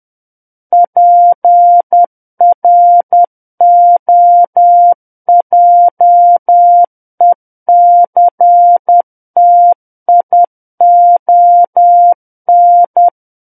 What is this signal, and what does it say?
Morse code, heard as a keyed tone, about 10 wpm, 700 Hz. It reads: PROJECTION
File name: Book One / 01-Chapter 1 - 10 WPM